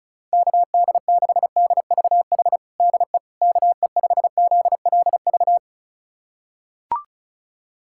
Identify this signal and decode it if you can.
Morse code, heard as a keyed tone, about 34 wpm, 700 Hz. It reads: KD6DVH DE KE5ZLV
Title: USA Callsign DE Callsign 35wpm